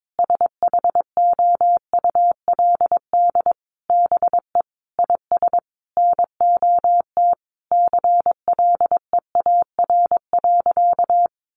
Morse code, heard as a keyed tone, about 22 wpm, 700 Hz. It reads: SHOULD BE IS NOT CLEAR.
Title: Sentences from Top 1300 Words 22wpm